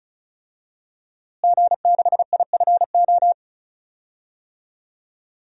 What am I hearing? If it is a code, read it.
Morse code, heard as a keyed tone, about 34 wpm, 700 Hz. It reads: G6IFO
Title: DXCC-100 Call Signs - 5 Characters - Repeated 3x - 35wpm